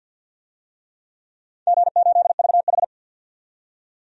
Morse code, heard as a keyed tone, about 52 wpm, 700 Hz. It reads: G8VH
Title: DXCC-100 Call Signs - 4 Characters - 50wpm